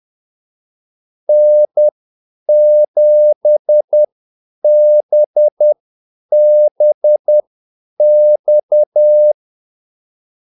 Morse code, heard as a keyed tone, about 10 wpm, spaced as for 8 wpm, 600 Hz. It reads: N7BBX
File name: Callsigns-10-8wpm